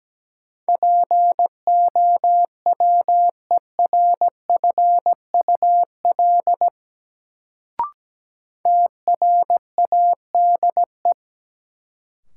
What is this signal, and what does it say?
Morse code, heard as a keyed tone, about 17 wpm, 700 Hz. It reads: POWERFUL TRADE
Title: Sets of 5 Words Encore - New Word Drill - 17wpm